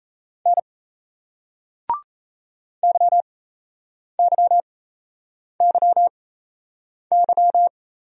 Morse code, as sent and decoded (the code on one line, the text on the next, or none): -. / -.-- -.-- -.-- -.--
N YYYY